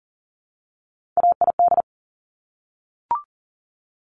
Morse code, heard as a keyed tone, about 41 wpm, 690 Hz. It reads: AID
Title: Top Three Letter Combinations in Words 40wpm